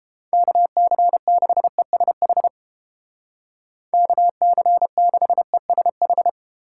Morse code, heard as a keyed tone, about 29 wpm, 700 Hz. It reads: KC6ESH KC6ESH